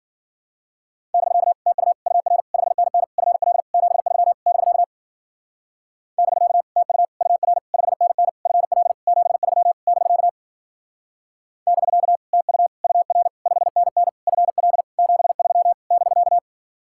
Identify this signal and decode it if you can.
Morse code, heard as a keyed tone, about 50 wpm, 700 Hz. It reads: <BK> TU UR 5NN FL 73 <BK> <BK> TU UR 5NN FL 73 <BK> <BK> TU UR 5NN FL 73 <BK>
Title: Speed-Racing Common QSO Phrases - 40wpm